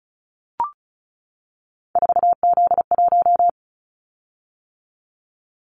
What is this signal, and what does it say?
Morse code, heard as a keyed tone, about 34 wpm, 700 Hz. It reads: 4Z1
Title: Callsign Prefixes for Top 50 DXCC - 35wpm